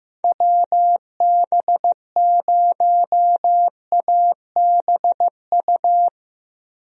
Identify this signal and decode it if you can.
Morse code, heard as a keyed tone, about 15 wpm, 700 Hz. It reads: WB0ABU
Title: USA Call Signs - 6 Characters - Repeated 3x - 15wpm